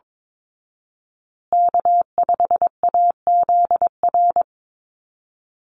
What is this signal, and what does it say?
Morse code, heard as a keyed tone, about 22 wpm, 700 Hz. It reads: K5AZR
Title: Call Signs Repeated 2x - 22wpm